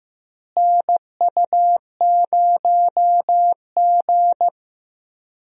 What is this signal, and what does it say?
Morse code, heard as a keyed tone, about 15 wpm, 700 Hz. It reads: NU0G
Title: USA Call Signs - 4 Characters - Repeated 2x - 15wpm